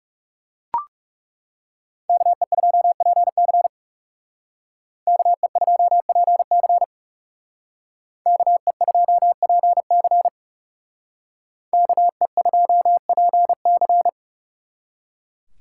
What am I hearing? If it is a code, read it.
Morse code, heard as a keyed tone, about 38 wpm, 700 Hz. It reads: KE2PC KE2PC KE2PC KE2PC